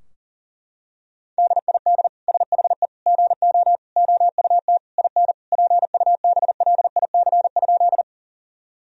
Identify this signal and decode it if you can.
Morse code, heard as a keyed tone, about 40 wpm, 700 Hz. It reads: DID SHE GO OUT IN PUBLIC?
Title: Sentences from Top 1000 Words - Review Part 3 - 40wpm